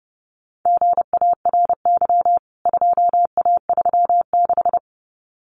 Morse code, heard as a keyed tone, about 30 wpm, 700 Hz. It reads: GARY 2A36
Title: CWT Exchange 30wpm